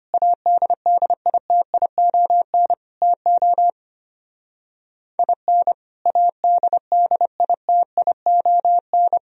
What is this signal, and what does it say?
Morse code, heard as a keyed tone, about 32 wpm, 700 Hz. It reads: ADDITION TO IN ADDITION